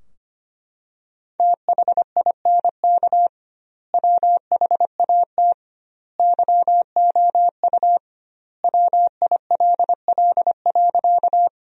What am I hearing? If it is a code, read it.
Morse code, heard as a keyed tone, about 25 wpm, 700 Hz. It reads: THINK WHAT YOU WILL.
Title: 2x Word-Spacing - Sentences from Top 200 Words - 25wpm